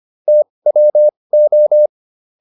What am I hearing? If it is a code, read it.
Morse code, heard as a keyed tone, about 25 wpm, spaced as for 20 wpm, 600 Hz. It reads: TWO